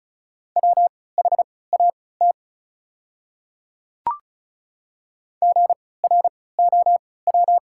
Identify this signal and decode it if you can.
Morse code, heard as a keyed tone, about 36 wpm, spaced as for 20 wpm, 690 Hz. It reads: WHAT GROW
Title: 3x Character Spacing - Top 300 Words - 35wpm